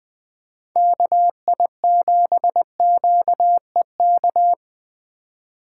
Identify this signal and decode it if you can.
Morse code, heard as a keyed tone, about 20 wpm, 700 Hz. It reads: KI7QEK
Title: 2023 SOTA Chaser Call Signs - 20wpm